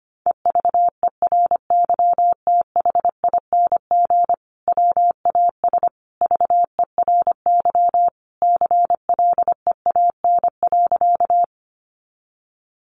Morse code, heard as a keyed tone, about 25 wpm, 700 Hz. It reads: EVERYTHING WAS VERY CLEAN.
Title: Sentences from Top 1100 Words 25wpm